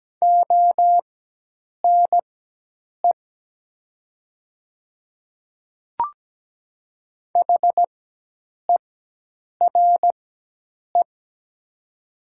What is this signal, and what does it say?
Morse code, heard as a keyed tone, about 17 wpm, spaced as for 8 wpm, 700 Hz. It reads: ONE HERE
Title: 4x Character Spacing - Top 200 Words - 17wpm